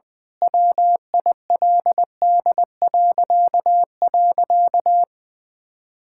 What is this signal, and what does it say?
Morse code, heard as a keyed tone, about 20 wpm, 700 Hz. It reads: WILD..
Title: Pangram Challenge v2 - 20wpm